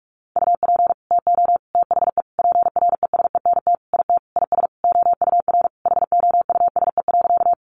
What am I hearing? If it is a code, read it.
Morse code, heard as a keyed tone, about 44 wpm, 700 Hz. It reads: UP TO THE PRESENT IT IS OUR HOUSE.
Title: Sentences from Top 800 Words 45wpm